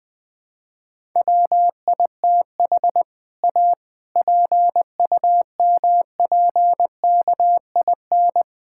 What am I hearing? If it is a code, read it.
Morse code, heard as a keyed tone, about 20 wpm, 700 Hz. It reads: WITH A PUMPKIN
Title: All-Age Appropriate Jokes - 20wpm